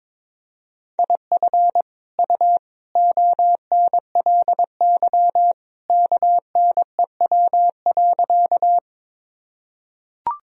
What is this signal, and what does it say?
Morse code, heard as a keyed tone, about 22 wpm, 700 Hz. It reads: IF U ONLY KNEW.
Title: Sentences from Top 400 Words and Abbreviations 22wpm